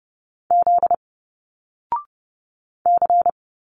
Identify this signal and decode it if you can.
Morse code, heard as a keyed tone, about 30 wpm, 700 Hz. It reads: Z C
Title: Single Letters 30wpm